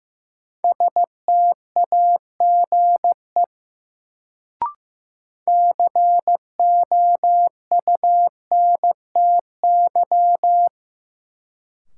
Morse code, heard as a keyed tone, about 16 wpm, 700 Hz. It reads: STAGE COUNTY
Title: Sentences from Top 1200 Words - New Word Drill - 15wpm